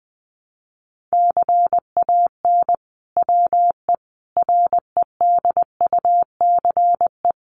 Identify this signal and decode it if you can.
Morse code, heard as a keyed tone, about 20 wpm, 700 Hz. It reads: CAN WE REDUCE
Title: Sentences from Top 1100 Words 20wpm